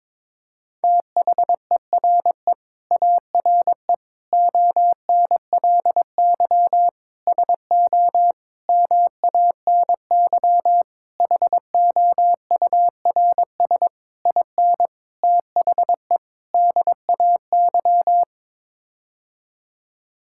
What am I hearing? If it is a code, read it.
Morse code, heard as a keyed tone, about 22 wpm, 700 Hz. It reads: THERE ARE ONLY SO MANY HOURS IN THE DAY
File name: Sets of 9 Words - 22wpm